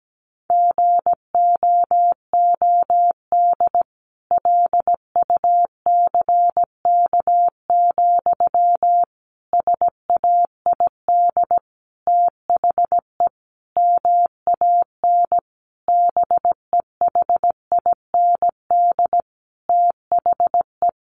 Morse code, heard as a keyed tone, about 17 wpm, 700 Hz. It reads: GOOD LUCK, SAID THE MAN BEHIND THE
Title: A Sound Of Thunder 17wpm